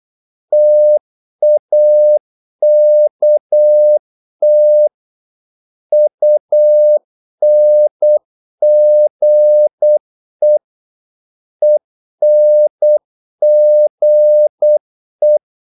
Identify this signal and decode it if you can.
Morse code, heard as a keyed tone, about 8 wpm, 600 Hz. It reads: TAKT UNGE ENGE
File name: Korte ord DK 8wpm